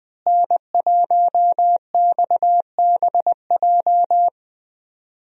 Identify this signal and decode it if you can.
Morse code, heard as a keyed tone, about 20 wpm, 700 Hz. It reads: N1XBJ
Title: USA Call Signs - 5 Characters - Repeated 2x - 20wpm